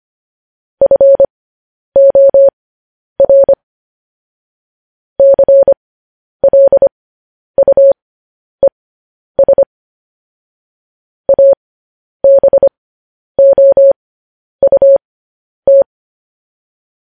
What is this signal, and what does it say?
Morse code, heard as a keyed tone, about 25 wpm, spaced as for 10 wpm, 550 Hz. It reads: FOR CLUES ABOUT
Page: News Headlines in Morse Code at 10 WPM